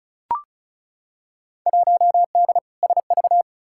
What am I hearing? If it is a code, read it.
Morse code, heard as a keyed tone, about 34 wpm, 700 Hz. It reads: 1D SV
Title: Field Day Exchanges - 35wpm